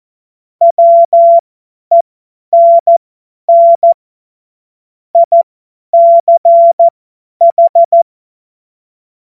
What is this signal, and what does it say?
Morse code, heard as a keyed tone, about 14 wpm, spaced as for 10 wpm, 680 Hz. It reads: WENN ICH
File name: Morsekurs / Witze / Witz 03. Mai 2022 50 F70